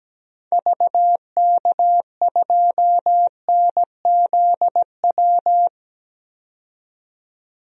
Morse code, heard as a keyed tone, about 17 wpm, 700 Hz. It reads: VK2NZW
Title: DXCC-100 Call Signs - 6 Characters - 17wpm